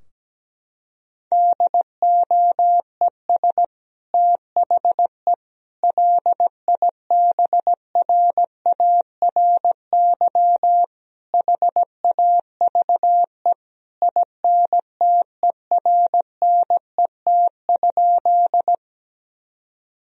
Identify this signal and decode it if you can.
Morse code, heard as a keyed tone, about 17 wpm, 700 Hz. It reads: DOES THE LIBRARY HAVE INTERNET?
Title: Sentences from Top 1000 Words - Review Part 4 - 17wpm